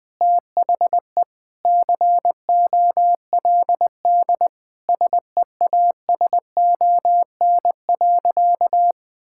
Morse code, heard as a keyed tone, about 21 wpm, 700 Hz. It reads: THE COLD SEASON.
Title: Sentences with 5-Word Phrases - Spoken Beginning - 20wpm